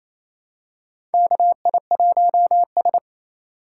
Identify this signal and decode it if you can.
Morse code, heard as a keyed tone, about 28 wpm, 700 Hz. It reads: KI1S
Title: USA Call Signs - 4 Characters - 3x Character Spacing - 28wpm